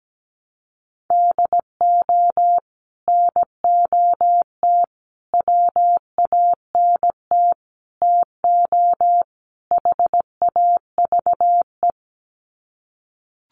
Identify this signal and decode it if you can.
Morse code, heard as a keyed tone, about 17 wpm, 700 Hz. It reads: DO NOT WANT TO HAVE
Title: Sets of 5 Words 17wpm